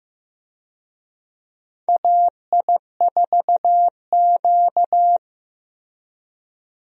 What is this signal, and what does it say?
Morse code, heard as a keyed tone, about 15 wpm, 700 Hz. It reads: AI4Q
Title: USA Call Signs - 4 Characters - Repeated 2x - 15wpm